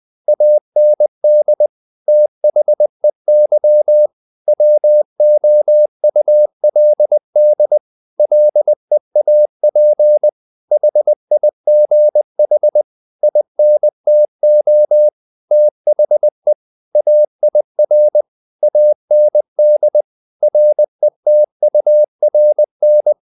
Morse code, heard as a keyed tone, about 20 wpm, 600 Hz. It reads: AND THEY WOULD LEAP HIGH INTO THE AIR AND RETURN
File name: the old man and the sea part 2 20wpm